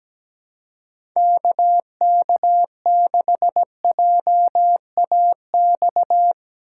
Morse code, heard as a keyed tone, about 17 wpm, 700 Hz. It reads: KK6JAX
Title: USA Call Signs - 6 Characters - 2x Character Spacing - 17wpm